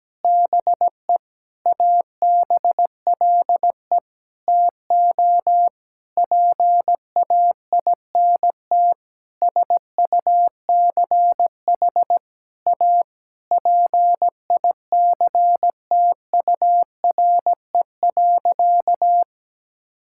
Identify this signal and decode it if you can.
Morse code, heard as a keyed tone, about 17 wpm, 700 Hz. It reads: BE ABLE TO PAINT SUCH A PICTURE.
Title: Sentences from Top 1000 Words - Review Part 6 - 17wpm